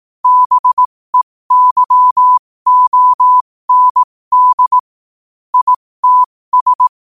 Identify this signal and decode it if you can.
Morse code, heard as a keyed tone, about 18 wpm, spaced as for 15 wpm, 1000 Hz. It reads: BEYOND ITS
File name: qotdfiles / 15wpm